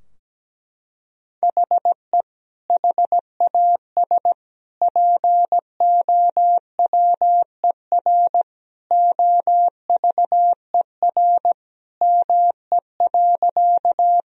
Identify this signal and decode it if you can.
Morse code, heard as a keyed tone, about 17 wpm, 700 Hz. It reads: HE HAS POWER OVER ME.
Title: Sentences from Top 1000 Words - Review Part 2 - 17wpm